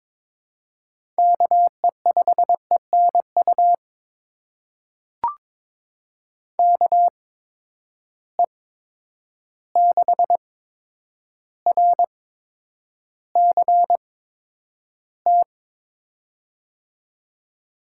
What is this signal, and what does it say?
Morse code, heard as a keyed tone, about 22 wpm, spaced as for 9 wpm, 700 Hz. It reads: KE5ENU KE6RCT
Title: USA Call Signs - 6 Characters - 8x Character Spacing - 22wpm